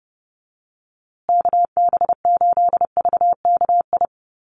Morse code, heard as a keyed tone, about 30 wpm, 700 Hz. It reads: KB8VKI
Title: USA Call Signs Repeated 3x 30wpm